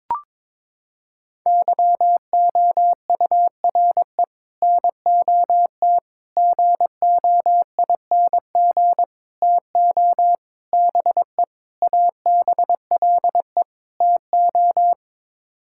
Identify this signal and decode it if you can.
Morse code, heard as a keyed tone, about 22 wpm, 700 Hz. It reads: YOURE NOT GOING TO BE ABLE TO
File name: practice / 22wpm / Sets of 7 Words - 22wpm